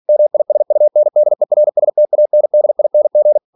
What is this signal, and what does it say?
Morse code, heard as a keyed tone, about 46 wpm, 600 Hz. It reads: MISUNDERSTANDING
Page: Never 47wpm